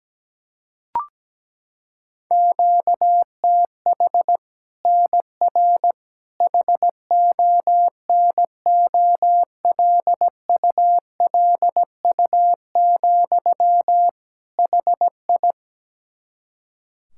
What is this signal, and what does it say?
Morse code, heard as a keyed tone, about 17 wpm, 700 Hz. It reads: QTH NR HONOLULU, HI
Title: Practice QSO - Exchange 1 - Beginning - 17wpm